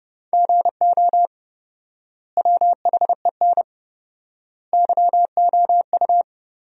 Morse code, heard as a keyed tone, about 32 wpm, 700 Hz. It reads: GO WHEN YOU
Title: 4x Word-Spacing - Sentences from Top 500 Words - 30wpm